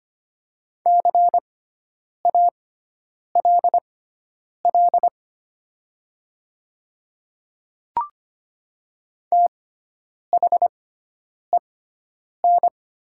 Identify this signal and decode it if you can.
Morse code, heard as a keyed tone, about 25 wpm, spaced as for 9 wpm, 700 Hz. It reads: CALL THEN
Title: 6x Character Spacing - Top 100 Words - 25wpm